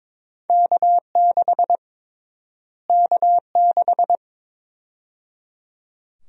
Morse code, heard as a keyed tone, about 22 wpm, 700 Hz. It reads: K6 K6
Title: USA Call Sign Prefixes - Repeated 2x - 22wpm